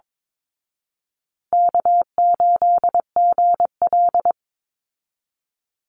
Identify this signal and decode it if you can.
Morse code, heard as a keyed tone, about 22 wpm, 700 Hz. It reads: K8GL
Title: Easy Call Signs Repeated 2x - 22wpm